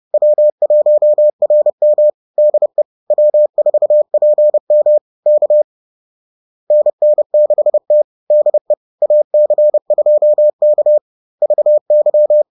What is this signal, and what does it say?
Morse code, heard as a keyed tone, about 30 wpm, 600 Hz. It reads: W1RM DE W4PM K NN6T DE AC2K VY